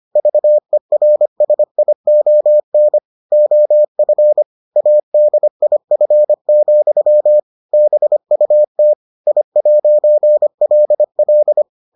Morse code, heard as a keyed tone, about 25 wpm, 600 Hz. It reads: VERSION OF ADIF, BUT I'LL